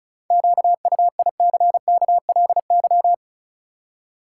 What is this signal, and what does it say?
Morse code, heard as a keyed tone, about 36 wpm, 700 Hz. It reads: QUICKLY
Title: Seven Letter Words - 35wpm